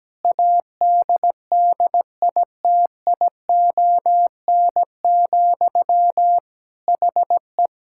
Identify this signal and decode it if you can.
Morse code, heard as a keyed tone, about 17 wpm, 700 Hz. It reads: ADDITION, HE
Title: Sentences from Top 1000 Words - Review Part 6 - 17wpm